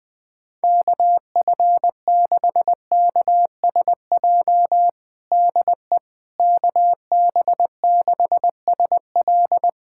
Morse code, heard as a keyed tone, about 21 wpm, 700 Hz. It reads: KF6KSJ DE KB6SL
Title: USA Callsign DE Callsign 20wpm